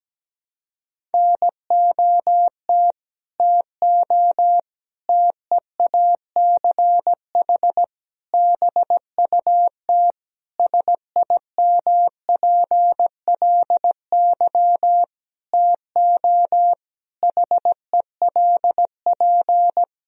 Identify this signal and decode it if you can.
Morse code, heard as a keyed tone, about 17 wpm, 700 Hz. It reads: NOT TO TEACH BUT SIMPLY TO HELP
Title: Sentences from Top 1000 Words - Review Part 4 - 17wpm